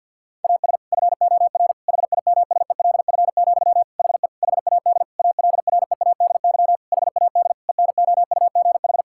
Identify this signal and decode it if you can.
Morse code, heard as a keyed tone, about 50 wpm, 700 Hz. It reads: AS FOR HIMSELF, HE HAD ALREADY HAD ENOUGH
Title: Sentences from Top 1600 Words 50wpm